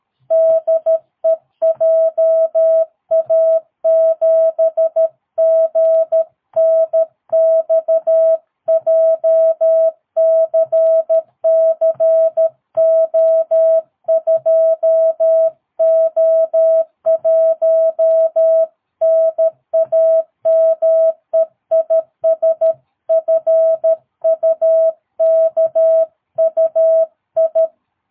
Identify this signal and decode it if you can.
Morse code, heard as a keyed tone, about 13 wpm, 660 Hz. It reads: DE JA7GNX JCCO2O1 NAME IS FUKUI
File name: jh8cbh / 2022cw